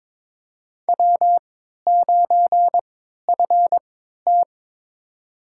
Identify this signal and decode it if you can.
Morse code, heard as a keyed tone, about 22 wpm, spaced as for 12 wpm, 700 Hz. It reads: W9FT
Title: USA Call Signs - 4 Characters - 3x Character Spacing - 22wpm